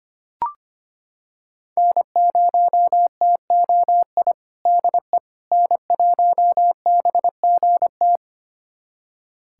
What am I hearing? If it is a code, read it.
Morse code, heard as a keyed tone, about 25 wpm, 700 Hz. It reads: N0TOI DE N1BGT
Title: USA Callsign DE Callsign - 5 Characters - 25wpm